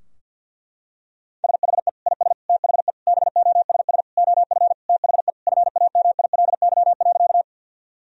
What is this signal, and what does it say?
Morse code, heard as a keyed tone, about 52 wpm, 700 Hz. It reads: SHE IS THE BOSS OF THE FAMILY.
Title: Sentences from Top 1000 Words - Review Part 4 - 50wpm